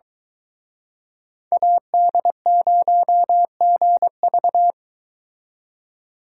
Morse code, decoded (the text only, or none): AD0GV